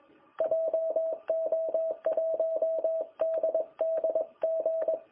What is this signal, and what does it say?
Morse code, heard as a keyed tone, about 20 wpm, 660 Hz. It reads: JO1DDG
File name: jh8cbh / 2021cw